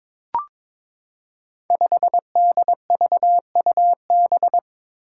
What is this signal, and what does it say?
Morse code, heard as a keyed tone, about 22 wpm, 700 Hz. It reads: 5DVUB
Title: Five Challenging Letter-Number Combinations 22wpm